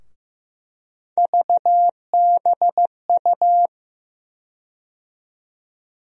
Morse code, heard as a keyed tone, about 15 wpm, 700 Hz. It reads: VBU
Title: USA Call Sign Postfixes - 3 Letters 15wpm